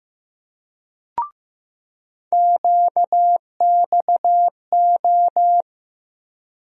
Morse code, heard as a keyed tone, about 15 wpm, 700 Hz. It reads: QXO
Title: USA Call Sign Postfixes - 3 Letters 15wpm